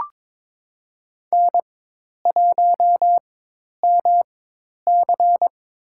Morse code, heard as a keyed tone, about 22 wpm, spaced as for 10 wpm, 700 Hz. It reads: N1MC
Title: USA Call Signs - 4x Character Spacing - 22wpm